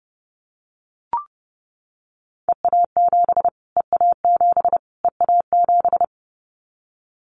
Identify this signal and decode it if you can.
Morse code, heard as a keyed tone, about 30 wpm, 690 Hz. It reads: EA7 EA7 EA7
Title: Callsign Prefixes for Top 25 DXCC - Repeated 3x - 30wpm